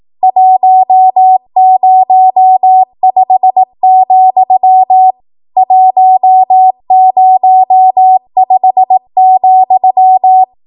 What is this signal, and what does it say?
Morse code, heard as a keyed tone, about 18 wpm, 750 Hz. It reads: 105, 105,
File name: Morse / 18WPM